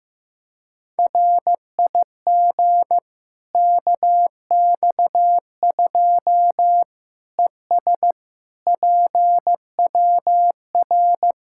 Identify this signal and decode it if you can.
Morse code, heard as a keyed tone, about 15 wpm, 700 Hz. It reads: RIG KX2 ES PWR
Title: K1USN SST QSO Practice 15wpm